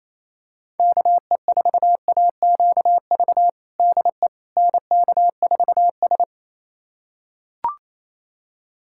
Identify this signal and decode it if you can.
Morse code, heard as a keyed tone, about 28 wpm, 700 Hz. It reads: KE4AQV DE NK4S
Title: USA Callsign DE Callsign 28wpm